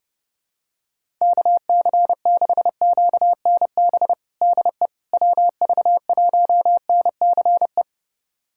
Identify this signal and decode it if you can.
Morse code, heard as a keyed tone, about 30 wpm, 700 Hz. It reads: KC6QNB DE WV1NCE
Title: USA Callsign DE Callsign - 6 Characters - 30wpm